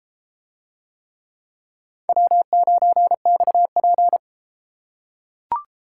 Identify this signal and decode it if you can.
Morse code, heard as a keyed tone, about 32 wpm, 700 Hz. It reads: W9XP